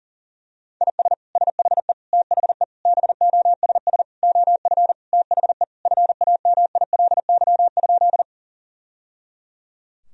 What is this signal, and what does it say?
Morse code, heard as a keyed tone, about 40 wpm, 700 Hz. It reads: IS SHE THE BOSS OF THE FAMILY?
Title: Sentences from Top 1000 Words - Review Part 4 - 40wpm